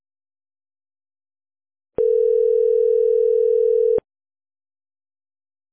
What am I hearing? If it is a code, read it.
Ringback tone (North America), 440 + 480 Hz.